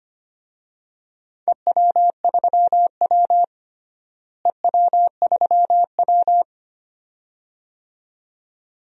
Morse code, heard as a keyed tone, about 25 wpm, 700 Hz. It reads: EW3W EW3W
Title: DXCC-100 Call Signs - 4 Characters - Repeated 2x - 25wpm